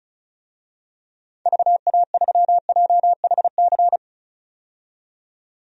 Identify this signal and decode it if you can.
Morse code, heard as a keyed tone, about 36 wpm, 700 Hz. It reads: VA3JHC
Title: DXCC-100 Call Signs - 6 Characters - 35wpm